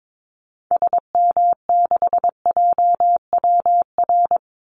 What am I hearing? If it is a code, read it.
Morse code, heard as a keyed tone, about 22 wpm, 700 Hz. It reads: SM6JWR
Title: Call Signs Repeated 3x - 22wpm